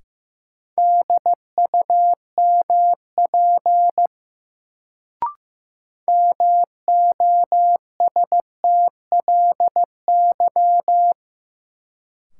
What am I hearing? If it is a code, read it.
Morse code, heard as a keyed tone, about 15 wpm, 700 Hz. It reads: DUMP MOSTLY
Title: Sentences from Top 1300 Words - New Word Drill - 15wpm